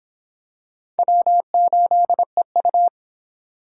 Morse code, heard as a keyed tone, about 25 wpm, 700 Hz. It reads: W8EU